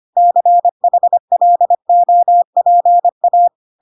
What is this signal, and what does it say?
Morse code, heard as a keyed tone, about 25 wpm, 700 Hz. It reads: CHLOPA